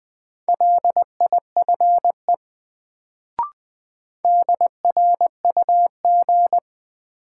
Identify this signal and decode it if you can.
Morse code, heard as a keyed tone, about 20 wpm, 700 Hz. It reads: LIFE DRUG
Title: Sets of 5 Words - New Word Drill - 20wpm